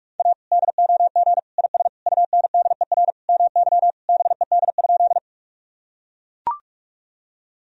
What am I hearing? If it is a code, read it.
Morse code, heard as a keyed tone, about 43 wpm, 700 Hz. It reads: A DOG IS UNDER MY BED?
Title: Sentences from Top 1000 Words - Review Part 2 - 45wpm